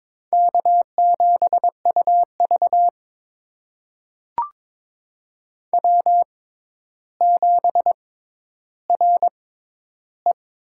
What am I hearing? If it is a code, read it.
Morse code, heard as a keyed tone, about 22 wpm, spaced as for 11 wpm, 700 Hz. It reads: K7UV W7RE
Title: USA Call Signs - 4 Characters - 6x Character Spacing - 22wpm